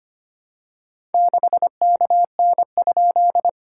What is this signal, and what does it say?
Morse code, heard as a keyed tone, about 25 wpm, 690 Hz. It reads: 6KN?
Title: SOTA QSO Practice 25wpm